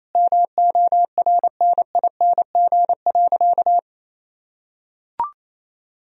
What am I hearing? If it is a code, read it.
Morse code, heard as a keyed tone, about 28 wpm, 700 Hz. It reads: MORNING.
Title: Sentences from Top 1000 Words - Review Part 2 - 28wpm